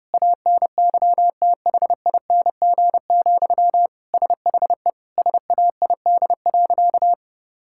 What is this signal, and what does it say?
Morse code, heard as a keyed tone, about 32 wpm, 700 Hz. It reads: ANYTHING, SHE SAID.
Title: Sentences from Top 1000 Words - Review Part 6 - 30wpm